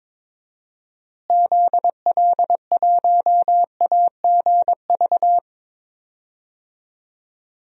Morse code, heard as a keyed tone, about 22 wpm, 700 Hz. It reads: ZL1AGV
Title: DXCC-100 Call Signs - 6 Characters - Repeated 3x - 22wpm